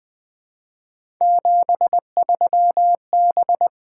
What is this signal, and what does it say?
Morse code, heard as a keyed tone, about 20 wpm, 700 Hz. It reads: 73B
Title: Three Letter-Number Combinations - 20wpm